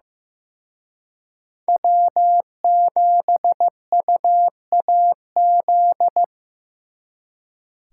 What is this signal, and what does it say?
Morse code, heard as a keyed tone, about 18 wpm, 700 Hz. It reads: W7UAZ
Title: Speed Racing - USA 1x3 Call Signs 15wpm